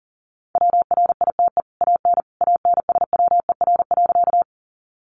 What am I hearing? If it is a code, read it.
Morse code, heard as a keyed tone, about 40 wpm, 700 Hz. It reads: WRITE AN ANSWER.
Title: Sentences from Top 300 Words 40wpm